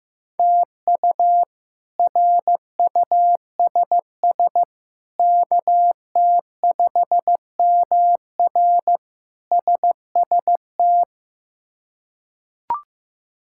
Morse code, heard as a keyed tone, about 15 wpm, 700 Hz. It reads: TU RUSS KT5MR SST
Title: K1USN SST QSO Practice - 15wpm